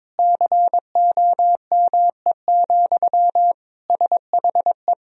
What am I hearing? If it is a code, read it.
Morse code, heard as a keyed tone, about 22 wpm, 700 Hz. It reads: COME, SHE
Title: Sentences from Top 100 Words and Abbreviations 22wpm